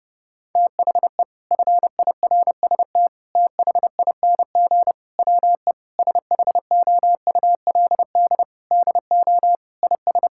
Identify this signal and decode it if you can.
Morse code, heard as a keyed tone, about 30 wpm, 700 Hz. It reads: THE FIRST THING WE SHOULD DO IS
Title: Sentences from Top 1000 Words - Review Part 6 - 30wpm